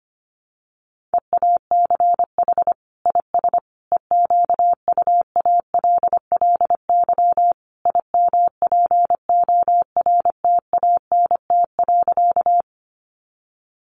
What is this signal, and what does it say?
Morse code, heard as a keyed tone, about 25 wpm, 700 Hz. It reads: EACH IS EQUALLY IMPORTANT.
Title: Sentences from Top 1400 Words 25wpm